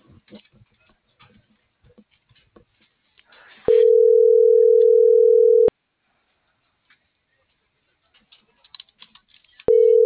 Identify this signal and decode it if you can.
Ringback tone (North America), 440 + 480 Hz.